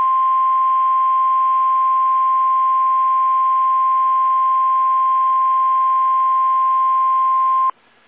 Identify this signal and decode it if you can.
Emergency Alert System (SAME) tones (the 1050 Hz attention tone). It is heard from 0.5 to 7.2 s.
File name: eas / KSDZKDJL / KSDZKDJL